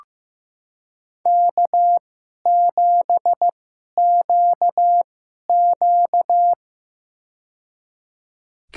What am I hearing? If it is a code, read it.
Morse code, heard as a keyed tone, about 15 wpm, spaced as for 11 wpm, 700 Hz. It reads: K7QQ
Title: USA Call Signs - 4 Characters - 2x Character Spacing - 15wpm